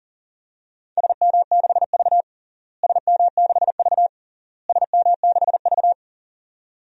Morse code, heard as a keyed tone, about 40 wpm, 700 Hz. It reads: SM6V SM6V SM6V
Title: DXCC-100 Call Signs - 4 Characters - Repeated 3x - 40wpm